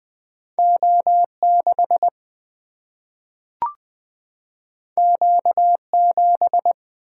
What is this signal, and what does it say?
Morse code, heard as a keyed tone, about 20 wpm, 700 Hz. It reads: O6 Q7
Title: Single-Letter Single-Number 20wpm